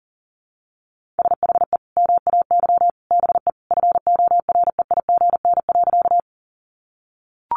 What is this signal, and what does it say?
Morse code, heard as a keyed tone, about 40 wpm, 700 Hz. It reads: SHE MAY BE FOREIGN.
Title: Sentences from Top 1400 Words 40wpm